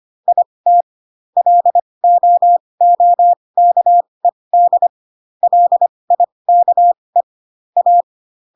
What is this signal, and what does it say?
Morse code, heard as a keyed tone, about 25 wpm, spaced as for 20 wpm, 700 Hz. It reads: IT LOOKED LIKE A
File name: Pinocchio / Pinocchio ch1 15 25 ik1hge